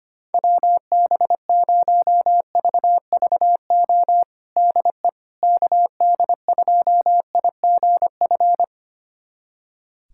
Morse code, heard as a keyed tone, about 25 wpm, 700 Hz. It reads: WB0VVO DE KD2IGF
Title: USA Callsign DE Callsign - 6 Characters - 25wpm